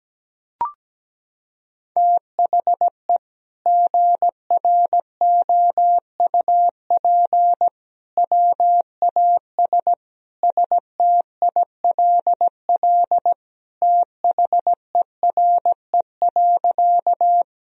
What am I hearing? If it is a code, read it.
Morse code, heard as a keyed tone, about 17 wpm, 700 Hz. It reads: THE GROUP WAS STILL THERE.
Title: Sentences from Top 1000 Words - Review Part 1 - 17wpm